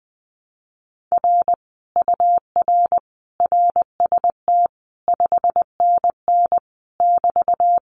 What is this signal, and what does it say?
Morse code, heard as a keyed tone, about 19 wpm, 700 Hz. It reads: R UR RST 5NN <BT>
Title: Common QSO Phrases Encore 20wpm